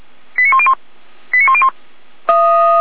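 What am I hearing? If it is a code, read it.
Five-tone selective call (ZVEI), heard as quick runs of five tones, tones of about 70 ms. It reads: 89101 89101
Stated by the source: ZVEI five-tone selective call 89092, received on SDR.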